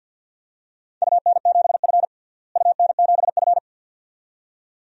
Morse code, heard as a keyed tone, about 52 wpm, 700 Hz. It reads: UN7F UN7F
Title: DXCC-100 Call Signs - 4 Characters - Repeated 2x - 50wpm